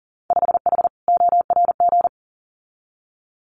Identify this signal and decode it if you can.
Morse code, heard as a keyed tone, about 40 wpm, 700 Hz. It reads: 5H ORG
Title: Winter Field Day Exchanges Encore 40wpm